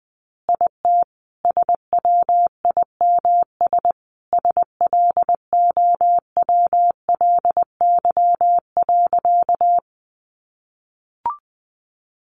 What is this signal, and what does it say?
Morse code, heard as a keyed tone, about 20 wpm, 700 Hz. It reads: IT SWIMS SLOWLY.
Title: Additional Easy Phrases 20wpm